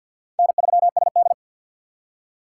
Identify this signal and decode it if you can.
Morse code, heard as a keyed tone, about 50 wpm, 700 Hz. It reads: N3SD
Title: USA Call Signs - 4 Characters - 2x Character Spacing - 50wpm